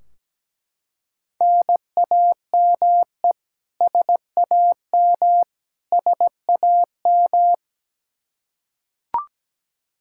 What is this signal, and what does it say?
Morse code, heard as a keyed tone, about 17 wpm, 700 Hz. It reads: NAME SAM SAM
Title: Practice QSO - Exchange 1 - Beginning - 17wpm